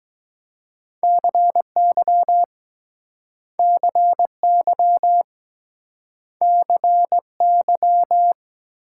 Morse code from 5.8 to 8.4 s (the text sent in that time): CY